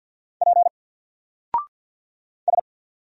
Morse code, heard as a keyed tone, about 48 wpm, 700 Hz. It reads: P S
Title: Single Character (Letter:Number:Punctuation:Prosign) Alt 50wpm